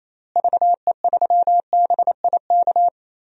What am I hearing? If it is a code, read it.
Morse code, heard as a keyed tone, about 28 wpm, 700 Hz. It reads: VE3BIK
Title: DXCC-100 Call Signs - 6 Characters - Repeated 3x - 28wpm